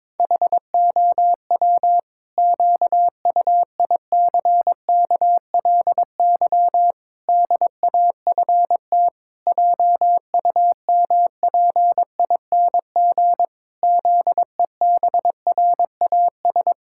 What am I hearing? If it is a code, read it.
Morse code, heard as a keyed tone, about 22 wpm, 700 Hz. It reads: HOW QUICKLY DAFT JUMPING ZEBRAS
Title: Pangram Challenge v2 - 22wpm